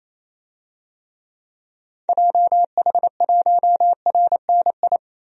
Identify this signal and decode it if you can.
Morse code, heard as a keyed tone, about 28 wpm, 700 Hz. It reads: JH1RNI
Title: DXCC-100 Call Signs - 6 Characters - 28wpm